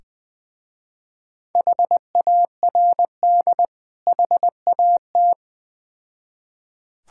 Morse code, heard as a keyed tone, about 20 wpm, 700 Hz. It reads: HARD HAT
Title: Difficult Words with Hints - 20wpm